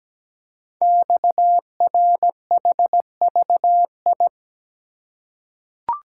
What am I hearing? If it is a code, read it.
Morse code, heard as a keyed tone, about 17 wpm, 700 Hz. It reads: XRHVI
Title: Code Groups - 5 Letters - 17wpm